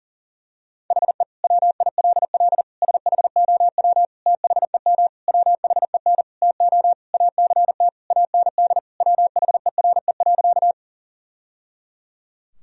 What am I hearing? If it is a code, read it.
Morse code, heard as a keyed tone, about 40 wpm, 700 Hz. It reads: HE WILL SHOW THEM WHEN TO ACT AND WHERE.
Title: Sentences from Top 1000 Words - Review Part 1 - 40wpm